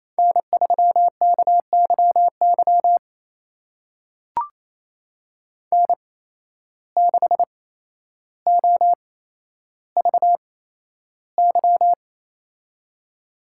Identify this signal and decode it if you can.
Morse code, heard as a keyed tone, about 28 wpm, spaced as for 12 wpm, 700 Hz. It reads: N3KYY N6OVY
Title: USA Call Signs - 5 Characters - 8x Character Spacing - 28wpm